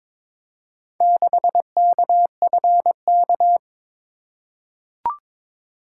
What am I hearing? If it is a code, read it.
Morse code, heard as a keyed tone, about 22 wpm, 700 Hz. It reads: 6KFK
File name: practice / 22wpm / Four Letter-Number Combinations - 22wpm